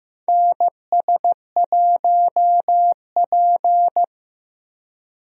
Morse code, heard as a keyed tone, about 16 wpm, 700 Hz. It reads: NS1P
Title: USA Call Signs - 4 Characters - Repeated 3x - 15wpm